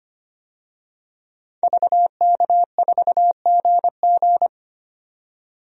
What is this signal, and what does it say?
Morse code, heard as a keyed tone, about 25 wpm, 700 Hz. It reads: VK4GG
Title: DXCC-100 Call Signs - 5 Characters - 25wpm